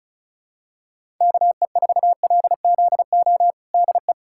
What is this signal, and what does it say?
Morse code, heard as a keyed tone, about 36 wpm, 700 Hz. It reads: KE4LZO DE
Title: USA Callsign DE Callsign - 6 Characters - 35wpm